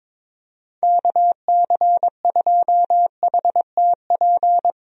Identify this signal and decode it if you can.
Morse code, heard as a keyed tone, about 21 wpm, 700 Hz. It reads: KC2HTP
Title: USA Call Signs - 6 Characters - 22wpm